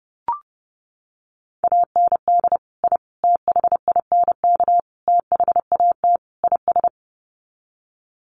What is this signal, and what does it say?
Morse code, heard as a keyed tone, about 30 wpm, 700 Hz. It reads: AND I THINK THAT IS
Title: Sets of 5 Words Encore2 30wpm